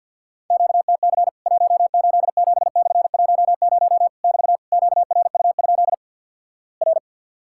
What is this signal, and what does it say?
Morse code, heard as a keyed tone, about 50 wpm, 700 Hz. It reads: QTC 187/10 <BT> QRU? R
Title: Worked All Europe DX Contest - QTC Exchange with Repeats - 50wpm